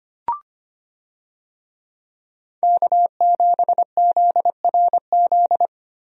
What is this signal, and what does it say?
Morse code, heard as a keyed tone, about 25 wpm, 700 Hz. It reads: K7ZRZ
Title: USA Call Signs - 5 Characters - Repeated 2x - 25wpm